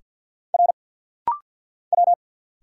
Morse code, heard as a keyed tone, about 50 wpm, 700 Hz. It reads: R W
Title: Single Letter-Number - Rapid-Fire 50wpm